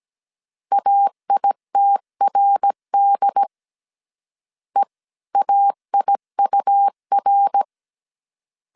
Morse code, heard as a keyed tone, about 17 wpm, 800 Hz. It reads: AITRD EAIUR